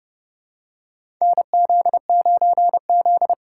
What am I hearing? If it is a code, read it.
Morse code, heard as a keyed tone, about 30 wpm, 700 Hz. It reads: NZ9Z
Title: USA Call Signs - 4 Characters - 4x Character Spacing - 30wpm